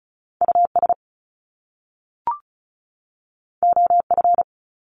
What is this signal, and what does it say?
Morse code, heard as a keyed tone, about 36 wpm, 700 Hz. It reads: US OF
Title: Top Two Letter Combinations in Words 35wpm